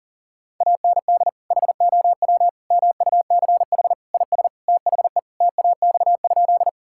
Morse code, heard as a keyed tone, about 40 wpm, 700 Hz. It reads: AND HOW MUCH IS THE TAX?
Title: Sentences from Top 1000 Words - Review Part 5 - 40wpm